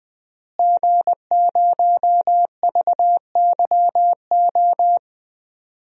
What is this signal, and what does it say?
Morse code, heard as a keyed tone, about 20 wpm, 700 Hz. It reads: G0VYO
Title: DXCC-100 Call Signs - 5 Characters - Repeated 3x - 20wpm